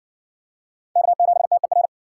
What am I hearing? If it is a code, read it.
Morse code, heard as a keyed tone, about 54 wpm, 700 Hz. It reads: K7TER